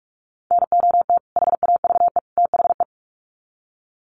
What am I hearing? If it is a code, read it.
Morse code, heard as a keyed tone, about 44 wpm, 700 Hz. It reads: NOT HAVE THE
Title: Sets of 3 Words 45wpm